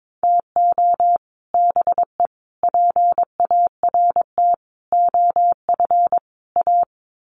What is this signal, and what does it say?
Morse code, heard as a keyed tone, about 22 wpm, 700 Hz. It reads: TO BE PART OF A
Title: Sets of 5 Words Encore2 22wpm